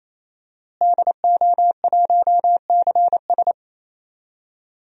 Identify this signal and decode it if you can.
Morse code, heard as a keyed tone, about 28 wpm, 700 Hz. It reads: DO1CS
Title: DXCC-100 Call Signs - 5 Characters - Repeated 3x - 28wpm